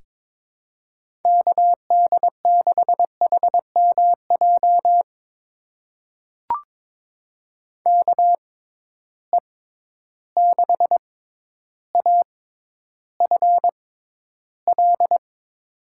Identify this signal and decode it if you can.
Morse code, heard as a keyed tone, about 22 wpm, spaced as for 11 wpm, 700 Hz. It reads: KD6HMJ KE6AFL
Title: USA Call Signs - 6 Characters - 6x Character Spacing - 22wpm